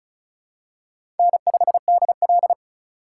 Morse code, heard as a keyed tone, about 34 wpm, 700 Hz. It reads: N5DL
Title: USA Call Signs - 2x Character Spacing - 35wpm